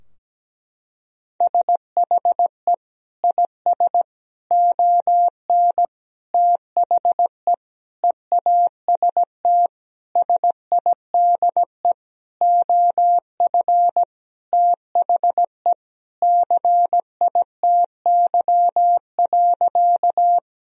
Morse code, heard as a keyed tone, about 17 wpm, 700 Hz. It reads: SHE IS ON THE EAST SIDE OF THE CITY.
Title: Sentences from Top 1000 Words - Review Part 3 - 17wpm